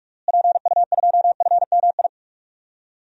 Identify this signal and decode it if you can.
Morse code, heard as a keyed tone, about 47 wpm, 700 Hz. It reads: PU2FMI
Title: DXCC-100 Call Signs - 6 Characters - Repeated 3x - 45wpm